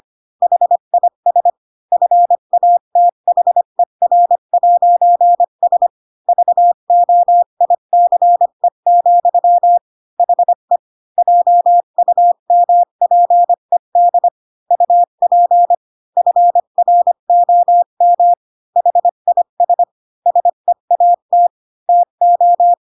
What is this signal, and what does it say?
Morse code, heard as a keyed tone, about 25 wpm, 700 Hz. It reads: HIS FATHER'S VOICE, HE JUMPED UP FROM HIS SEAT TO
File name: Pinocchio / Pinocchio ch7 20 25 ik1hge